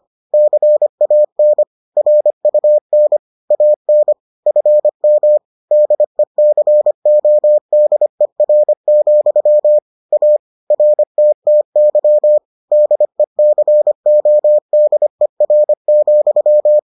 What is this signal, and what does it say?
Morse code, heard as a keyed tone, about 25 wpm, 600 Hz. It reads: CAN RUN AN FM DECODER, A RTTY DECODER,